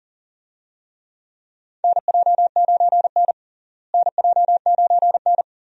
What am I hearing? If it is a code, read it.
Morse code, heard as a keyed tone, about 40 wpm, 700 Hz. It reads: NJ9N NJ9N
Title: USA Call Signs - 4 Characters - Repeated 2x - 40wpm